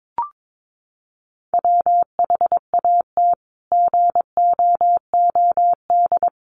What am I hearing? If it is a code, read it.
Morse code, heard as a keyed tone, about 22 wpm, 700 Hz. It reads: WHAT GOOD
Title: Sentences from Top 1300 Words 22wpm